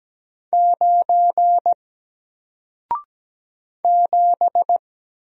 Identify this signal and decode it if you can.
Morse code, heard as a keyed tone, about 17 wpm, 700 Hz. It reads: 9 7
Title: Single Letter-Number Alt 17wpm